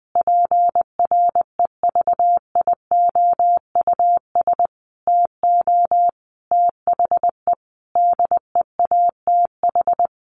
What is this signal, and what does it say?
Morse code, heard as a keyed tone, about 20 wpm, 700 Hz. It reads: PREVIOUS TO THE DEATH
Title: Sentences from Top 1500 Words 20wpm